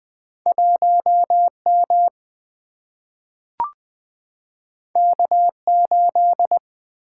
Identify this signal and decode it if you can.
Morse code, heard as a keyed tone, about 20 wpm, 700 Hz. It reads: 1M K8
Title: Single-Letter Single-Number 20wpm